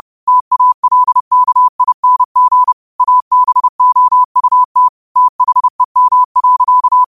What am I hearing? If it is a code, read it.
Morse code, heard as a keyed tone, about 30 wpm, 1000 Hz. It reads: TALKING ABOUT THEM.